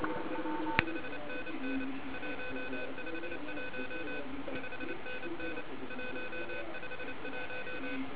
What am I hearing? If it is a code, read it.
Morse code, heard as a keyed tone, about 29 wpm, 2500 Hz, with noise. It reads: 5NN 151 5NN 151
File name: ARCHIV / 40m cw1